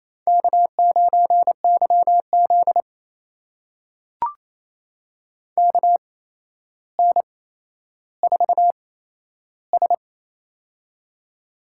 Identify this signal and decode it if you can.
Morse code, heard as a keyed tone, about 28 wpm, spaced as for 12 wpm, 700 Hz. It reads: K9YZ KN4S
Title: USA Call Signs - 4 Characters - 8x Character Spacing - 28wpm